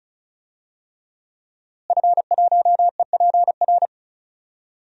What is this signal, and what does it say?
Morse code, heard as a keyed tone, about 36 wpm, 700 Hz. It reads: F1EPR
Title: DXCC-100 Call Signs - 5 Characters - Repeated 3x - 35wpm